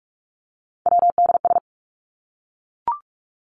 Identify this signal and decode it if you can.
Morse code, heard as a keyed tone, about 44 wpm, 700 Hz. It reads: WDS
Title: Abbreviations 45wpm